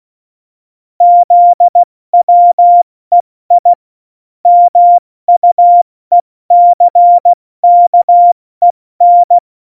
Morse code, heard as a keyed tone, about 16 wpm, spaced as for 14 wpm, 700 Hz. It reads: ZWEI MUECKEN
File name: Morsekurs / Witze / Witz 22. Mai 2018, 70 F80